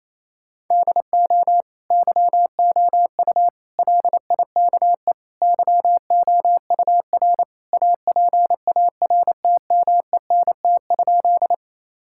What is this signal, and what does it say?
Morse code, heard as a keyed tone, about 27 wpm, 700 Hz. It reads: DO YOU LIKE YOUR APARTMENT?
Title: Sentences from Top 1000 Words - Review Part 6 - 28wpm